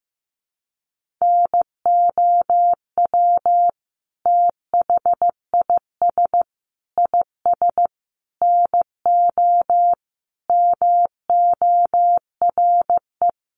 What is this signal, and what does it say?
Morse code, heard as a keyed tone, about 15 wpm, 700 Hz. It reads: NOW THIS IS NO MORE
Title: Sentences from Top 1400 Words 15wpm